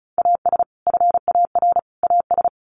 Morse code, heard as a keyed tone, about 34 wpm, 700 Hz. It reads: AS FAR AS
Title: Sets of 3 Words 35wpm